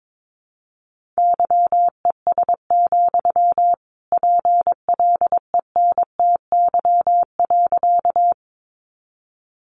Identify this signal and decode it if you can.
Morse code, heard as a keyed tone, about 22 wpm, 700 Hz. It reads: YES, PLENTY.
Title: Sentences from Top 1300 Words 22wpm